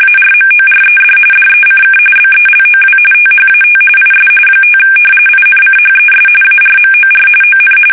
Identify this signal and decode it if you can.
RTTY, about 45.45 baud. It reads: ARMORED DECK BELOW THAT IS 5 INCHES ARMOR ON
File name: fsk850 / battleshipnj